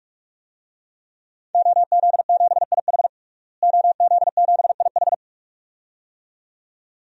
Morse code, heard as a keyed tone, about 44 wpm, 700 Hz. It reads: OZ7IH OZ7IH
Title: DXCC-100 Call Signs - 5 Characters - Repeated 2x - 45wpm